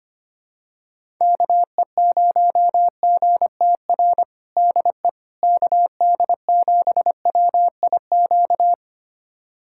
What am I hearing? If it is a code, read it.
Morse code, heard as a keyed tone, about 25 wpm, 700 Hz. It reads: KE0GTR DE KD7WIQ
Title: USA Callsign DE Callsign - 6 Characters - 25wpm